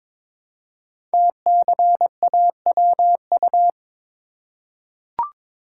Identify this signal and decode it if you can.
Morse code, heard as a keyed tone, about 22 wpm, 700 Hz. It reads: TCAWU
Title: Code Groups - 5 Letter:Numbers - 22wpm